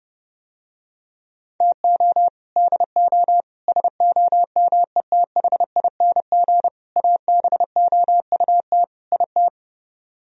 Morse code, heard as a keyed tone, about 30 wpm, 700 Hz. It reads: TO DO SOMETHING ABOUT IT